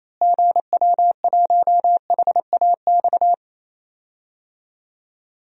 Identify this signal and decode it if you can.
Morse code, heard as a keyed tone, about 28 wpm, 700 Hz. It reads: GW1HAX
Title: DXCC-100 Call Signs - 6 Characters - Repeated 3x - 28wpm